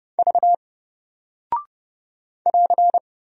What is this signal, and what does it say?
Morse code, heard as a keyed tone, about 30 wpm, 700 Hz. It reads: V <AR>
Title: Single Character (Letter:Number:Punctuation:Prosign) Alt 30wpm